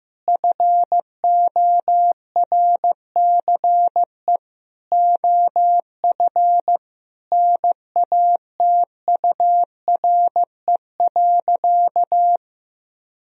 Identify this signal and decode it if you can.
Morse code, heard as a keyed tone, about 16 wpm, 700 Hz. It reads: FORCE OF NATURE.
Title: Sentences from Top 1000 Words - Review Part 4 - 15wpm